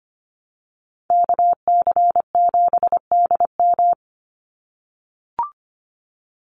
Morse code, heard as a keyed tone, about 25 wpm, 700 Hz. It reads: KC7DM
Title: NA SOTA Activator Callsigns 25wpm